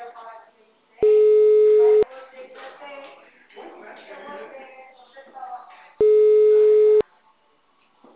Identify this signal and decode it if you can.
Ringback tone (Europe, 425 Hz), with speech.